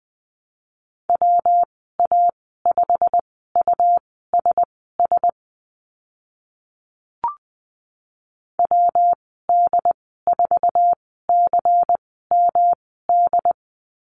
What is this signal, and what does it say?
Morse code, heard as a keyed tone, about 20 wpm, spaced as for 14 wpm, 700 Hz. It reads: WA5USS WD4CMD
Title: 179 Lesson 29g - Slash - Call Signs 20wpm